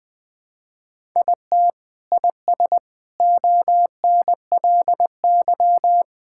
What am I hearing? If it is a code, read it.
Morse code, heard as a keyed tone, about 20 wpm, 700 Hz. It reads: IT IS ONLY
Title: Sentences with 7-Word Phrases - 20wpm